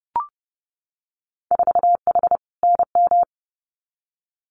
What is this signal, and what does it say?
Morse code, heard as a keyed tone, about 30 wpm, 700 Hz. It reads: 4H NM
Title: Winter Field Day Exchanges Encore 30wpm